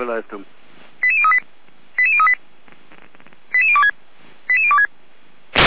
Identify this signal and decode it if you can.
Five-tone selective call (ZVEI), heard as quick runs of five tones, tones of about 70 ms, with speech. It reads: 88038 88038 88026 88026